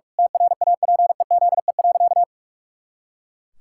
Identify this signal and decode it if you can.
Morse code, heard as a keyed tone, about 44 wpm, 700 Hz. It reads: TRAPEZE.
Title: Pangram Challenge v2 - 45wpm